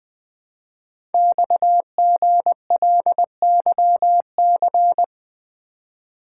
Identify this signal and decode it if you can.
Morse code, heard as a keyed tone, about 20 wpm, 700 Hz. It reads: XGLYC
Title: Code Groups - 5 Letters - 20wpm